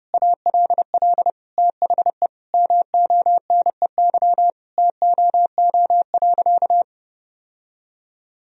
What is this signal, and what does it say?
Morse code, heard as a keyed tone, about 28 wpm, 700 Hz. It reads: ALL THE MONEY TOO.
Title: Sentences from Top 1000 Words - Review Part 4 - 30wpm